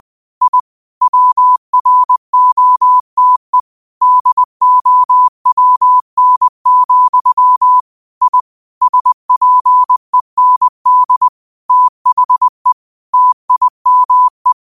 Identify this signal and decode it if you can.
Morse code, heard as a keyed tone, about 20 wpm, 1000 Hz. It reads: I WROTE DOWN, I SPEND THE TIME
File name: qotdfiles / 20wpm